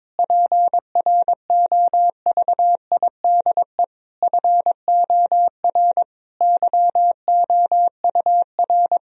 Morse code, heard as a keyed tone, about 21 wpm, 700 Hz. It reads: PROVIDE FOR YOUR
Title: Sentences from Top 1000 Words - Review Part 3 - 22wpm